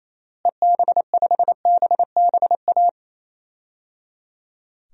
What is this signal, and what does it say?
Morse code, heard as a keyed tone, about 28 wpm, 700 Hz. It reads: EB5BBA
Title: DXCC-100 Call Signs - 6 Characters - Repeated 3x - 28wpm